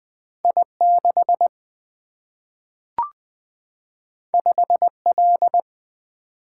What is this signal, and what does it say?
Morse code, heard as a keyed tone, about 20 wpm, 700 Hz. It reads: I 6 5 L
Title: Single-Letter Single-Number 20wpm